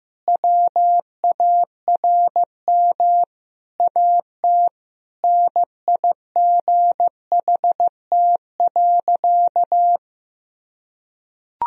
Morse code, heard as a keyed tone, about 15 wpm, 700 Hz. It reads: WARM AT NIGHT.
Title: Sentences from Top 1000 Words - Review Part 2 - 15wpm